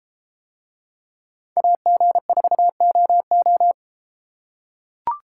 Morse code, heard as a keyed tone, about 32 wpm, 700 Hz. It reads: AG4OO